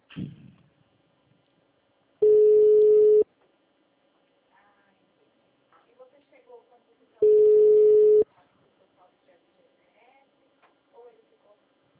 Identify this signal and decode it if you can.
Ringback tone (Europe, 425 Hz), with speech.